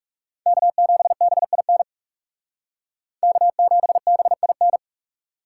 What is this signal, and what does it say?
Morse code, heard as a keyed tone, about 38 wpm, 700 Hz. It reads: K7BIN K7BIN